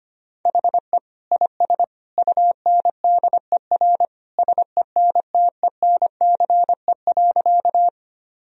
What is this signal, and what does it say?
Morse code, heard as a keyed tone, about 25 wpm, 700 Hz. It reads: HE IS UNDER SENTENCE.
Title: Sentences from Top 1000 Words - Review Part 1 - 25wpm